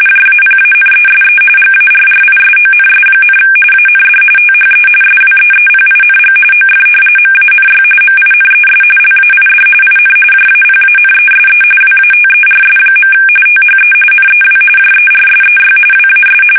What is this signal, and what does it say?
RTTY, about 45.45 baud. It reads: JAPANESE GARRISONS, AND THE SWEEPING OF MANY MINES LAID IN JAPANESE WATERS. THE NEW JERSEY WAS
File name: fsk850 / battleshipnj